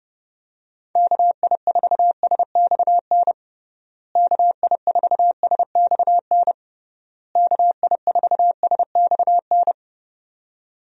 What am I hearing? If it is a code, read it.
Morse code, heard as a keyed tone, about 32 wpm, 700 Hz. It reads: KI4SXN KI4SXN KI4SXN
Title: USA Call Signs - 6 Characters - Repeated 3x - 30wpm